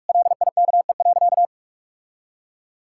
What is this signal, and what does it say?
Morse code, heard as a keyed tone, about 46 wpm, 700 Hz. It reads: LIKE.
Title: Sentences from Top 1600 Words 45wpm